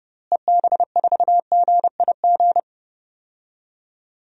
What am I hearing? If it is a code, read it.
Morse code, heard as a keyed tone, about 30 wpm, 700 Hz. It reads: EB4GIG
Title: DXCC-100 Call Signs - 6 Characters - Repeated 2x - 30wpm